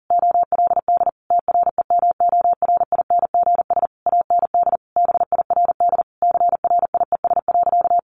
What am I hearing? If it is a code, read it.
Morse code, heard as a keyed tone, about 40 wpm, 700 Hz. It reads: OLD TREMORINGS AND BIRD CRIES.
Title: A Sound Of Thunder 40wpm